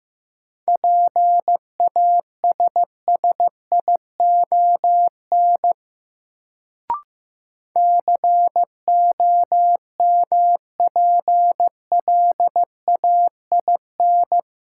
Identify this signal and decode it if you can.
Morse code, heard as a keyed tone, about 15 wpm, 700 Hz. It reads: PASSION COMPLAIN
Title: Sentences from Top 1300 Words - New Word Drill - 15wpm